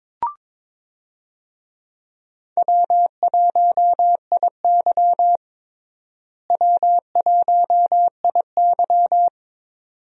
Morse code, heard as a keyed tone, about 22 wpm, 700 Hz. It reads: W1IY W1IY
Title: USA Call Signs - 4 Characters - Repeated 2x - 22wpm